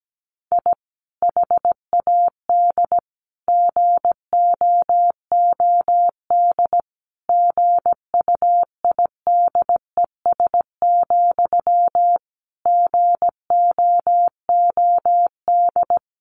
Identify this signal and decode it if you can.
Morse code, heard as a keyed tone, about 17 wpm, 700 Hz. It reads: I HAD GOOD GUIDES, GOOD
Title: A Sound Of Thunder 17wpm